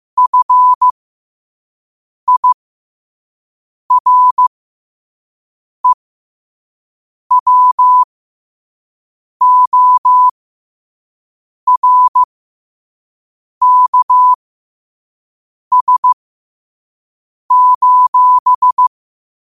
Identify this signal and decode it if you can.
Morse code, heard as a keyed tone, about 15 wpm, spaced as for 5 wpm, 1000 Hz. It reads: FIREWORKS:
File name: qotdfiles / 5wpm